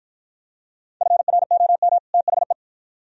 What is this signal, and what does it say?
Morse code, heard as a keyed tone, about 48 wpm, 700 Hz. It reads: FROM THE